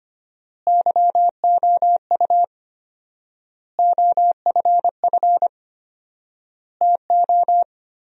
Morse code, heard as a keyed tone, about 25 wpm, 700 Hz. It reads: YOU OFF TO
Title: 4x Word-Spacing - Sentences from Top 400 Words - 25wpm